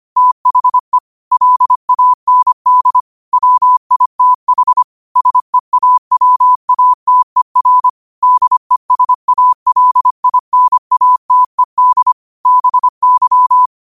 Morse code, heard as a keyed tone, about 25 wpm, 1000 Hz. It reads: THE LAND WITH SEAWATER DESALINATED BY
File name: qotdfiles / 25wpm